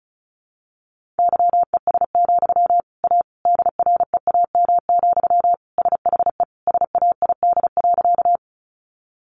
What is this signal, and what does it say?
Morse code, heard as a keyed tone, about 34 wpm, 700 Hz. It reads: YES, A DREAM, SHE SAID.
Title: Sentences from Top 1500 Words 35wpm